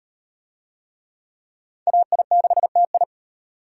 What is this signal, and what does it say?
Morse code, heard as a keyed tone, about 35 wpm, 700 Hz. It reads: AI6TI